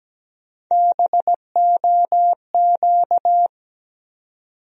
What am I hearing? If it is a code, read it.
Morse code, heard as a keyed tone, about 17 wpm, 700 Hz. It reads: BOQ
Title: USA Call Sign Postfixes - 3 Letters 17wpm